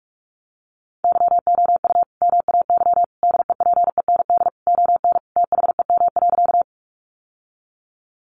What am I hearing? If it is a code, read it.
Morse code, heard as a keyed tone, about 44 wpm, 700 Hz. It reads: YOU MAY DEPEND ON THEM.
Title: Sentences from Top 1300 Words 45wpm